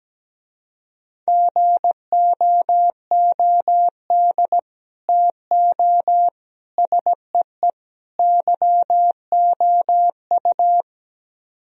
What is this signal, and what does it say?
Morse code, heard as a keyed tone, about 17 wpm, 700 Hz. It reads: GOOD TO SEE YOU
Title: Speed-Racing - Sets of 4 Words 17wpm